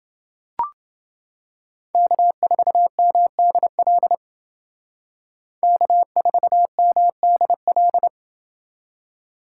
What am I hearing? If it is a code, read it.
Morse code, heard as a keyed tone, about 29 wpm, 700 Hz. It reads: K4MDL K4MDL